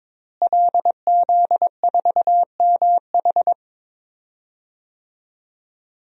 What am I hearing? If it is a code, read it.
Morse code, heard as a keyed tone, about 22 wpm, 700 Hz. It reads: LZ4MH
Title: DXCC-100 Call Signs - 5 Characters - Repeated 3x - 22wpm